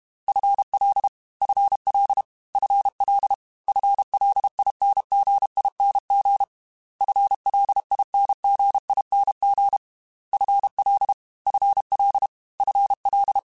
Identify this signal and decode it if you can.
Morse code, heard as a keyed tone, about 32 wpm, 800 Hz. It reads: FL FL FL FLINGING FLINGING FL FL FL
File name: FL Prefixes 3x2 @30